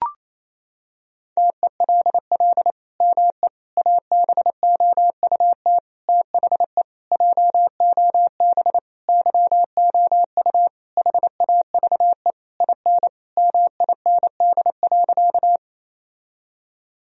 Morse code, heard as a keyed tone, about 27 wpm, 700 Hz. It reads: TELL ME ABOUT THE JOB YOU HAVE IN MIND.
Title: Sentences from Top 1000 Words - Review Part 3 - 28wpm